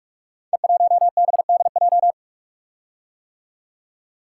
Morse code, heard as a keyed tone, about 44 wpm, 700 Hz. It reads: E1BDJ
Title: Code Groups - 5 Letter:Numbers - 45wpm